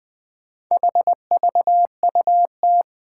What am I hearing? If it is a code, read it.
Morse code, heard as a keyed tone, about 20 wpm, 700 Hz. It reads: HVUT
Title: Four Challenging Letter-Number Combinations 20wpm